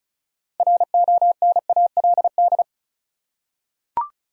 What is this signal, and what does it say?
Morse code, heard as a keyed tone, about 34 wpm, 700 Hz. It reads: RONALD
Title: USA Common Names - 35wpm